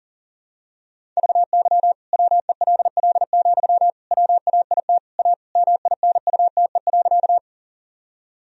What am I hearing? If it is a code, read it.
Morse code, heard as a keyed tone, about 40 wpm, 700 Hz. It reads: VY WELL, WAIT A MINUTE.
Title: Sentences from Top 500 Words and Abbreviations 40wpm